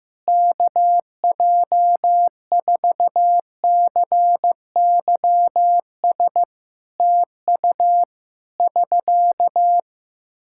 Morse code, heard as a keyed tone, about 15 wpm, 700 Hz. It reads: KJ4CYS TU <SK>
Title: K1USN SST QSO Practice 15wpm